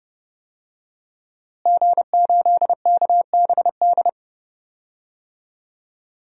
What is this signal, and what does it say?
Morse code, heard as a keyed tone, about 30 wpm, 700 Hz. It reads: G8KBD
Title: DXCC-100 Call Signs - 5 Characters - Repeated 2x - 30wpm